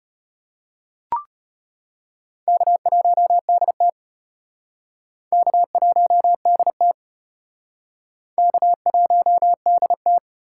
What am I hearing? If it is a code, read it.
Morse code, heard as a keyed tone, about 32 wpm, 700 Hz. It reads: K1DT K1DT K1DT